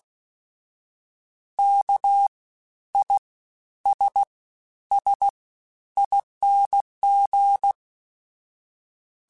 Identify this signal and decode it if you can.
Morse code, heard as a keyed tone, about 16 wpm, 800 Hz. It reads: K I S S ING
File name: Special INGs-mp3s / Special INGs-mp3s / Fast ing @15